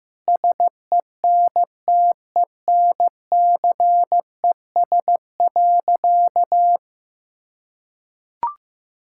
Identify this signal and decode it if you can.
Morse code, heard as a keyed tone, about 15 wpm, 700 Hz. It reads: SENTENCES.
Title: Pangram Challenge v2 - 15wpm